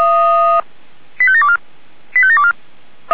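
Five-tone selective call (ZVEI), heard as quick runs of five tones, tones of about 70 ms. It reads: 85724 85724
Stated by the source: ZVEI five-tone selective call received via SDR.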